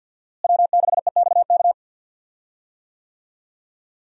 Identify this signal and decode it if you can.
Morse code, heard as a keyed tone, about 49 wpm, 700 Hz. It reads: W6EXK
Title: USA Call Signs - 5 Characters - Repeated 2x - 50wpm